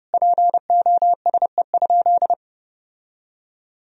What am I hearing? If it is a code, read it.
Morse code, heard as a keyed tone, about 30 wpm, 700 Hz. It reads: POSE?
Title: Sentences from Top 1000 Words - Review Part 2 - 30wpm